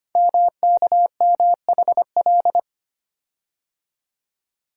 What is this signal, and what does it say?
Morse code, heard as a keyed tone, about 25 wpm, 700 Hz. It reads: MKMHL
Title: Code Groups - 5 Letters - 25wpm